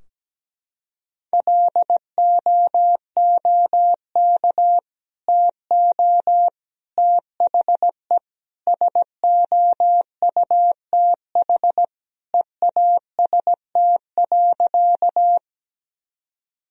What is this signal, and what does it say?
Morse code, heard as a keyed tone, about 17 wpm, 700 Hz. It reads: LOOK TO THE SOUTH EAST.
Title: Sentences from Top 1000 Words - Review Part 3 - 17wpm